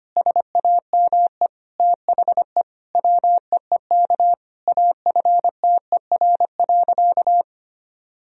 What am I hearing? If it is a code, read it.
Morse code, heard as a keyed tone, about 25 wpm, 700 Hz. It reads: SAME THE WEEK AFTER.
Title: Sentences from Top 500 Words and Abbreviations 25wpm